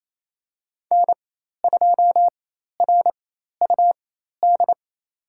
Morse code, heard as a keyed tone, about 27 wpm, spaced as for 13 wpm, 700 Hz. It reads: N2RUD
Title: USA Call Signs - 5 Characters - 4x Character Spacing - 28wpm